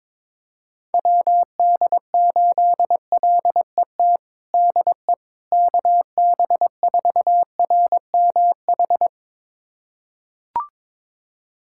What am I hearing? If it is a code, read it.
Morse code, heard as a keyed tone, about 22 wpm, 700 Hz. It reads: WD8LET DE KB4RMH
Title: USA Callsign DE Callsign 22wpm